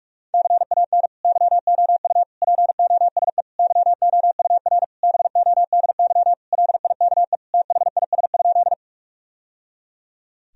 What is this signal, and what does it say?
Morse code, heard as a keyed tone, about 46 wpm, 700 Hz. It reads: CAN YOU POSE YOUR BODY LIKE THIS?
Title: Sentences from Top 1000 Words - Review Part 2 - 45wpm